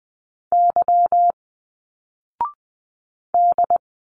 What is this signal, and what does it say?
Morse code, heard as a keyed tone, about 20 wpm, 700 Hz. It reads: Y D